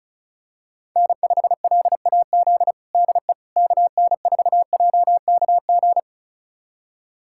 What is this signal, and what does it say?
Morse code, heard as a keyed tone, about 36 wpm, 700 Hz. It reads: N5LAZ DE KN4JKG
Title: USA Callsign DE Callsign 35wpm